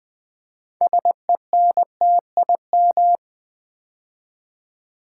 Morse code, heard as a keyed tone, about 20 wpm, 700 Hz. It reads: SENTIM
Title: Build A Word - 3 Letters At A Time - 9-Letter Words - 20wpm